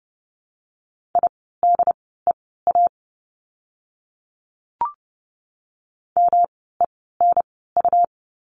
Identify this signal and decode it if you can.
Morse code, heard as a keyed tone, about 30 wpm, spaced as for 17 wpm, 700 Hz. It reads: IDEA MENU
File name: course / 30:10wpm / 108 Lesson 18f - Letter m - Words 30wpm